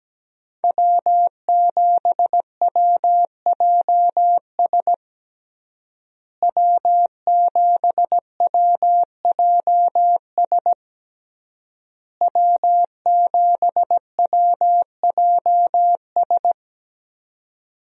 Morse code, heard as a keyed tone, about 17 wpm, 700 Hz. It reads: W7WJS W7WJS W7WJS
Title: USA Call Signs - 5 Characters - Repeated 3x - 17wpm